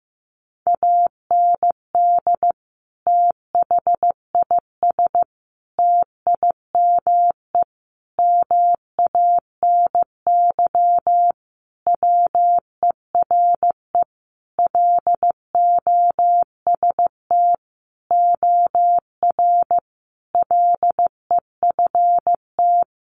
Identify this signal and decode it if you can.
Morse code, heard as a keyed tone, about 15 wpm, 700 Hz. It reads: AND THIS TIME MANY WERE LOST OR LEFT
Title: Sentences from Top 1400 Words 15wpm